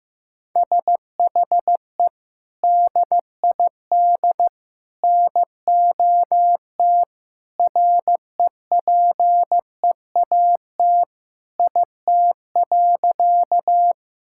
Morse code, heard as a keyed tone, about 15 wpm, 700 Hz. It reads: SHE DID NOT REPEAT IT.
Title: Sentences from Top 1600 Words 15wpm